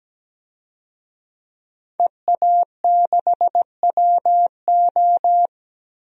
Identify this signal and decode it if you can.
Morse code, heard as a keyed tone, about 17 wpm, 700 Hz. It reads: EA6WO
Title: DXCC-100 Call Signs - 5 Characters - Repeated 3x - 17wpm